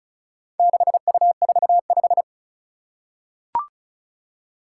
Morse code, heard as a keyed tone, about 36 wpm, 700 Hz. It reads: 6U45
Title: Four Challenging Letter-Number Combinations 35wpm